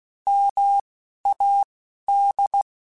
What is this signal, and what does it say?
Morse code, heard as a keyed tone, about 16 wpm, 800 Hz. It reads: M A D
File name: Alphabetical / Alphabetical mp3s / Ms @10